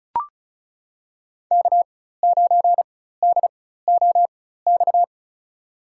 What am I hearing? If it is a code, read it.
Morse code, heard as a keyed tone, about 36 wpm, spaced as for 16 wpm, 700 Hz. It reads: K9DOX
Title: USA Call Signs - 5 Characters - 4x Character Spacing - 35wpm